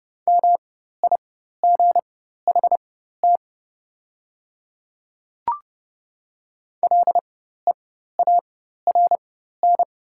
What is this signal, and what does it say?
Morse code, heard as a keyed tone, about 30 wpm, spaced as for 14 wpm, 700 Hz. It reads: MIGHT LEARN
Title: 4x Character Spacing - Top 500 Words - 30wpm